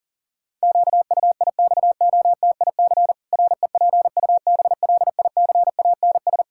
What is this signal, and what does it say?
Morse code, heard as a keyed tone, about 37 wpm, 700 Hz. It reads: QUIXOTIC REPUBLICANS
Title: Pangram Challenge v2 - 40wpm